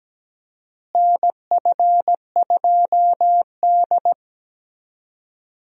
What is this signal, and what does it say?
Morse code, heard as a keyed tone, about 17 wpm, 700 Hz. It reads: NF2D
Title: USA Call Signs - 4 Characters - Repeated 3x - 17wpm